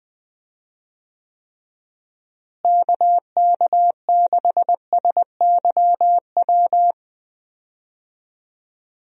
Morse code, heard as a keyed tone, about 21 wpm, 700 Hz. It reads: KK6SYW
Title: USA Call Signs - 6 Characters - Repeated 3x - 20wpm